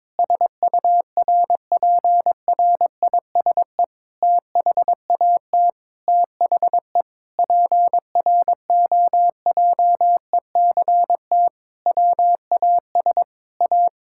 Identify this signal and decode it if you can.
Morse code, heard as a keyed tone, about 22 wpm, 700 Hz. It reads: SURPRISE THAT THE PROJECT WAS A
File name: Sentences with 9-Word Phrases - 22wpm